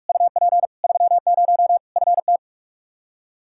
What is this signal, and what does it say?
Morse code, heard as a keyed tone, about 41 wpm, 700 Hz. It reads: UP 30 FT
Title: Practice QSO - Exchange 2 - Info - 45wpm